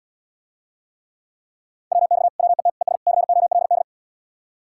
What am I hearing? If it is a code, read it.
Morse code, heard as a keyed tone, about 77 wpm, 700 Hz. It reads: WX WET ES COLD